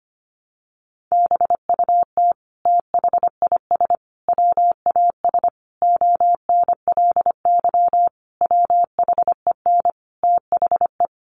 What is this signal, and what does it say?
Morse code, heard as a keyed tone, about 25 wpm, 700 Hz. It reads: BUT THIS WAS ONLY WHEN THE
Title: Sentences from Top 1300 Words 25wpm